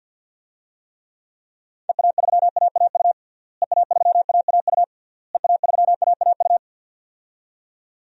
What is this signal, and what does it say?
Morse code, heard as a keyed tone, about 51 wpm, 700 Hz. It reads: EA3AAU EA3AAU EA3AAU
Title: DXCC-100 Call Signs - 6 Characters - Repeated 3x - 50wpm